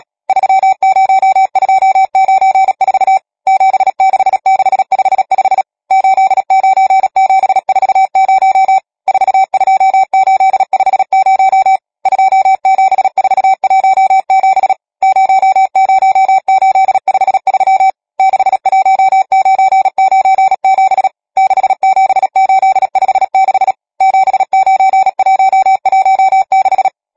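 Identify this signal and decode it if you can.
Morse code, heard as a keyed tone, about 37 wpm, 770 Hz. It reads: 30294 76655 89740 42850 27417 00853 61997 67856 79116
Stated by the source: Morse code number practice, machine-sent training text (Nácvik telegrafie, OK1FCB).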